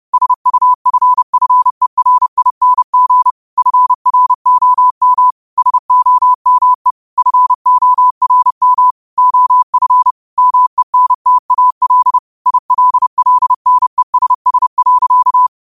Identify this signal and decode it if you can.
Morse code, heard as a keyed tone, about 30 wpm, 1000 Hz. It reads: SUFFERING FROM SOME FORM OF MENTAL ILLNESS.